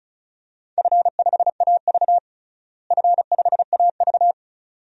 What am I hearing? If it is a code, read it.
Morse code, heard as a keyed tone, about 36 wpm, 700 Hz. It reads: F5AV F5AV
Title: DXCC-100 Call Signs - 4 Characters - Repeated 2x - 35wpm